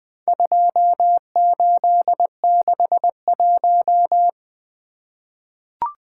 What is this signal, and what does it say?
Morse code, heard as a keyed tone, about 21 wpm, 700 Hz. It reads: 2861
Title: Four Letter-Number Combinations - 20wpm